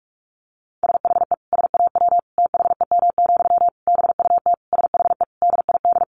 Morse code, heard as a keyed tone, about 44 wpm, 700 Hz. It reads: SHE SAW THEM, BUT SHE DID
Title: Sentences from Top 1000 Words 45wpm